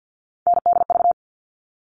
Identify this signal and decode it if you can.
Morse code, heard as a keyed tone, about 50 wpm, 700 Hz. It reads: NDV
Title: All Cut Numbers - 3-Digit 50wpm